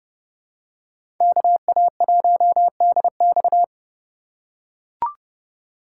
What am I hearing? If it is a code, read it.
Morse code, heard as a keyed tone, about 28 wpm, 700 Hz. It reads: KA1DX